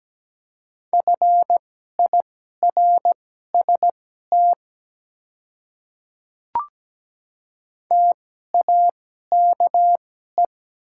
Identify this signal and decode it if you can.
Morse code, heard as a keyed tone, about 17 wpm, spaced as for 12 wpm, 700 Hz. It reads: FIRST TAKE
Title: 2x Character Spacing - Top 300 Words - 17wpm